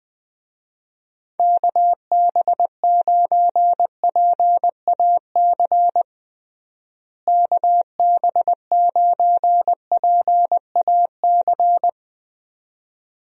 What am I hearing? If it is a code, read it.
Morse code, heard as a keyed tone, about 20 wpm, 700 Hz. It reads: KB9PAC KB9PAC
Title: USA Call Signs - 6 Characters - Repeated 2x - 20wpm